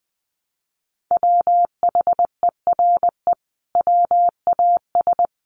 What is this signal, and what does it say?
Morse code, heard as a keyed tone, about 20 wpm, 700 Hz. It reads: WHERE WAS
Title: Sentences from Top 1200 Words 20wpm